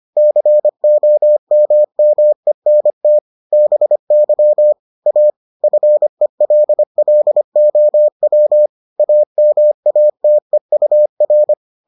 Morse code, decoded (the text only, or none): COMMENT BY A FELLOW AMATEUR